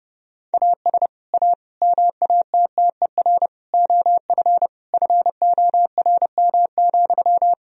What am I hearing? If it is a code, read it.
Morse code, heard as a keyed tone, about 32 wpm, 700 Hz. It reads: AS A MATTER OF FORM,
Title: Sentences from Top 1000 Words - Review Part 1 - 30wpm